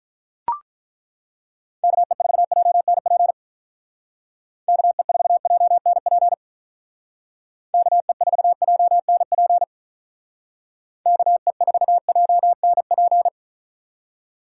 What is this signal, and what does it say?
Morse code, heard as a keyed tone, about 42 wpm, 700 Hz. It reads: KE4JNP KE4JNP KE4JNP KE4JNP
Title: Speed Racing - USA 2x3 Call Signs 35wpm